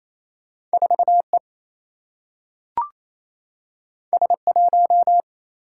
Morse code, heard as a keyed tone, about 28 wpm, 700 Hz. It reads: 4 E S 1
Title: Single-Letter Single-Number 28wpm